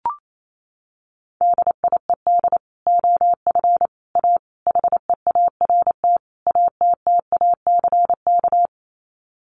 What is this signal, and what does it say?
Morse code, heard as a keyed tone, about 28 wpm, 700 Hz. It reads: DIED OF A HEART ATTACK
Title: Sets of 5 Words Encore2 28wpm